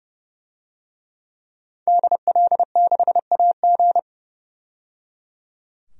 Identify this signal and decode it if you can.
Morse code, heard as a keyed tone, about 30 wpm, 700 Hz. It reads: DL6AG
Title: DXCC-100 Call Signs - 5 Characters - 30wpm